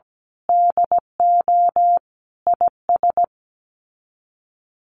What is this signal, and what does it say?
Morse code, heard as a keyed tone, about 17 wpm, 700 Hz. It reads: DO IS
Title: Sets of 5 Words Encore 17wpm